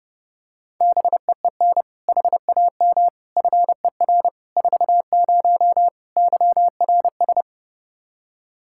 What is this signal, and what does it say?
Morse code, heard as a keyed tone, about 29 wpm, 700 Hz. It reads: BEEN HAM FER 40 YRS
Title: Practice QSO - Exchange 3 - More Info - 30wpm